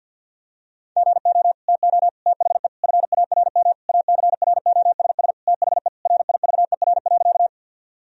Morse code, heard as a keyed tone, about 50 wpm, 700 Hz. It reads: GO TO THE FARM ACROSS THE RIVER.
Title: Sentences from Top 1000 Words - Review Part 2 - 50wpm